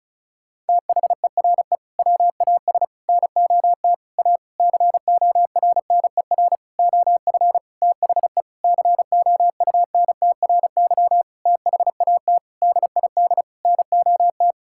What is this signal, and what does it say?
Morse code, heard as a keyed tone, about 34 wpm, 700 Hz. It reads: THERE WAS NOT A CORNER OF THE COUNTRY THAT DID NOT
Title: Sentences from Top 1600 Words 35wpm